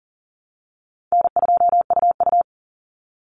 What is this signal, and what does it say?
Morse code, heard as a keyed tone, about 40 wpm, 700 Hz. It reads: N2UU
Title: Easy Call Signs - 40wpm